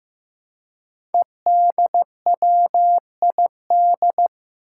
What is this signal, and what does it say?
Morse code, heard as a keyed tone, about 15 wpm, 700 Hz. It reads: EDWID
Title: Five Challenging Letter-Number Combinations 15wpm